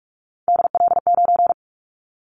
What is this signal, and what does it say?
Morse code, heard as a keyed tone, about 46 wpm, 700 Hz. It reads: DL9
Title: Callsign Prefixes for Top 25 DXCC - 45wpm